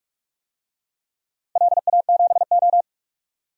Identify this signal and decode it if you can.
Morse code, heard as a keyed tone, about 48 wpm, 700 Hz. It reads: LA7O
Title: DXCC-100 Call Signs - 4 Characters - 45wpm